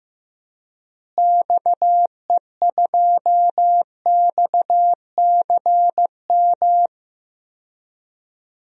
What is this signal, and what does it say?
Morse code, heard as a keyed tone, about 15 wpm, 700 Hz. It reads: XE2XCM
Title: DXCC-100 Call Signs - 6 Characters - Repeated 3x - 15wpm